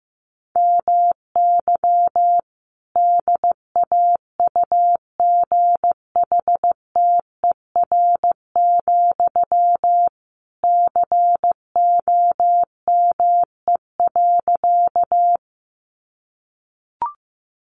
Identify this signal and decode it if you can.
Morse code, heard as a keyed tone, about 15 wpm, 700 Hz. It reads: MY DAUGHTER, COME.
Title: Sentences from Top 1500 Words 15wpm